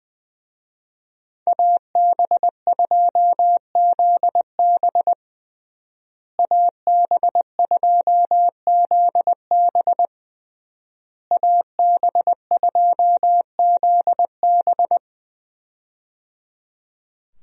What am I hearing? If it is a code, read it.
Morse code, heard as a keyed tone, about 20 wpm, 700 Hz. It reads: AB2ZB AB2ZB AB2ZB
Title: USA Call Signs - 5 Characters - Repeated 3x - 20wpm